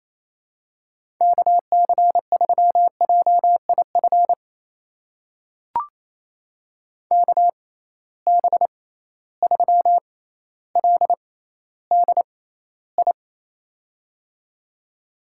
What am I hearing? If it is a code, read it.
Morse code, heard as a keyed tone, about 28 wpm, spaced as for 14 wpm, 700 Hz. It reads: KC3JIF KB3LDI
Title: USA Call Signs - 6 Characters - 6x Character Spacing - 28wpm